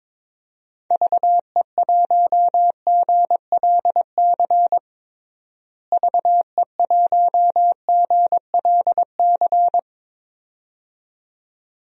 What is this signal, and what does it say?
Morse code, heard as a keyed tone, about 22 wpm, 700 Hz. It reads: VE1GLC VE1GLC
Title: DXCC-100 Call Signs - 6 Characters - Repeated 2x - 22wpm